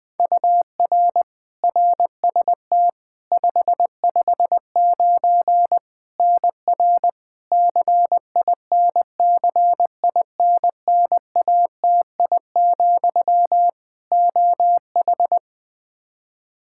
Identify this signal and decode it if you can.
Morse code, heard as a keyed tone, about 20 wpm, 700 Hz. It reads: UR RST 559 NR CINCINNATI, OH
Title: K1USN SST QSO Practice 20wpm